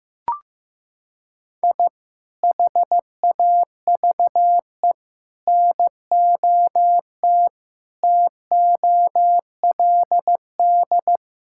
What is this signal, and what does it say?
Morse code, heard as a keyed tone, about 15 wpm, 700 Hz. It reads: I HAVE NOT TOLD
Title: Sentences from Top 1000 Words - Review Part 5 - 15wpm